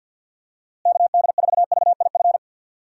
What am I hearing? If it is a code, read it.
Morse code, heard as a keyed tone, about 53 wpm, 700 Hz. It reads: KD4VIF
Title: USA Call Signs - 6 Characters - Repeated 3x - 50wpm